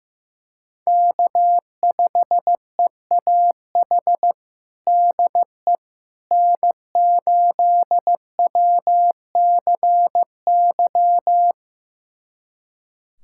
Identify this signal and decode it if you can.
Morse code, heard as a keyed tone, about 15 wpm, 700 Hz. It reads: K5EAH DE N8WCY
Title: USA Callsign DE Callsign - 5 Characters - 15wpm